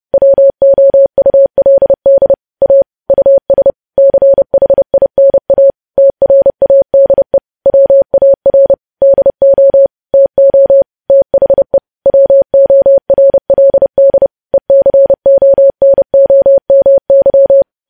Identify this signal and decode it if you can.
Morse code, heard as a keyed tone, about 30 wpm, 550 Hz. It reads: WOULD A US CHINA TRADE WAR DO TO THE WORLD ECONOMY
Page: news headlines in morse code at 30 wpm